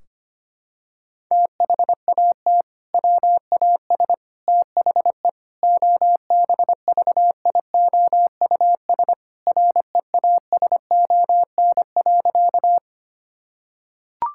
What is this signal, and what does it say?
Morse code, heard as a keyed tone, about 25 wpm, 700 Hz. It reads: THAT WAS THE OBVIOUS REASON.
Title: Sentences from Top 1600 Words 25wpm